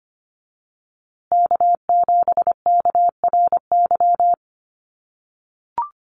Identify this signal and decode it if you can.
Morse code, heard as a keyed tone, about 25 wpm, 700 Hz. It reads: K7KRY
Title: USA Call Signs Repeated 2x 25wpm